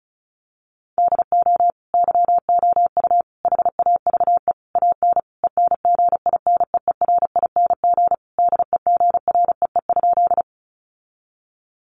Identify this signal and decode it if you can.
Morse code, heard as a keyed tone, about 34 wpm, 700 Hz. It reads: DO YOU HAVE AN ENGINEERING DEGREE?
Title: Sentences from Top 1500 Words 35wpm